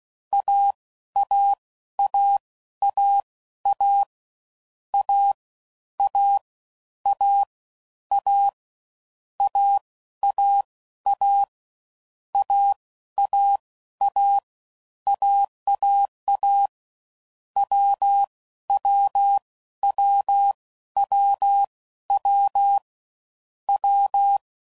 Morse code, heard as a keyed tone, about 16 wpm, spaced as for 12 wpm, 800 Hz. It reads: AAAAA A A A A AAA AAA AAA WWWWW W
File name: Combinations / Difficult Combos / AWJ1